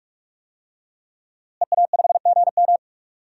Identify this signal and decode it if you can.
Morse code, heard as a keyed tone, about 41 wpm, 700 Hz. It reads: EA5GM
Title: DXCC-100 Call Signs - 5 Characters - Repeated 2x - 45wpm